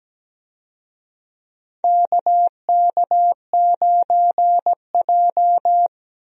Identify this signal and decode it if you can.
Morse code, heard as a keyed tone, about 17 wpm, 700 Hz. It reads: KK9J
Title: USA Call Signs - 4 Characters - 17wpm